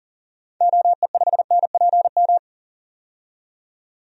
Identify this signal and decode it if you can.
Morse code, heard as a keyed tone, about 40 wpm, 700 Hz. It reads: OE5NPM
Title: DXCC-100 Call Signs - 6 Characters - 40wpm